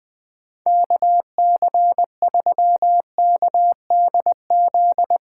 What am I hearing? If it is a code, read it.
Morse code, heard as a keyed tone, about 21 wpm, 700 Hz. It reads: KC3KDZ
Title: USA Callsign DE Callsign - 6 Characters - 20wpm